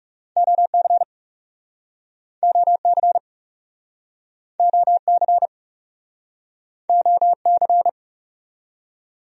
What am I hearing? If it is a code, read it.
Morse code, heard as a keyed tone, about 41 wpm, 700 Hz. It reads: OC OC OC OC